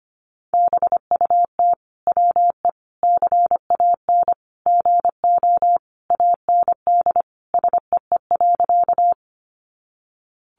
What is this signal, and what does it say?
Morse code, heard as a keyed tone, about 25 wpm, 700 Hz. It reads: BUT WE CAN GO AND SEE.
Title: Sentences from Top 100 Words 25wpm